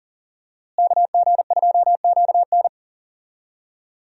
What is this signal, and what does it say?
Morse code, heard as a keyed tone, about 40 wpm, 700 Hz. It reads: KG2QN
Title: USA Call Signs - 5 Characters - Repeated 2x - 40wpm